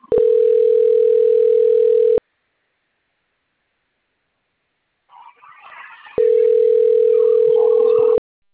Ringback tone (North America), 440 + 480 Hz.